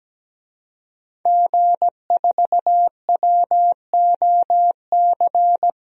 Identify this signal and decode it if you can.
Morse code, heard as a keyed tone, about 17 wpm, 700 Hz. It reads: G4WOC
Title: DXCC-100 Call Signs - 5 Characters - 17wpm